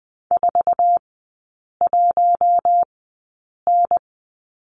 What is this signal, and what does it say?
Morse code, heard as a keyed tone, about 20 wpm, spaced as for 8 wpm, 700 Hz. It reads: 41N
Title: 047 Lesson 9e - Number 4 - Three Characters 20wpm